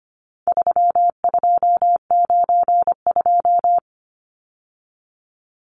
Morse code, heard as a keyed tone, about 25 wpm, 700 Hz. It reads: 3292
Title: Numbers - 4-Digit 25wpm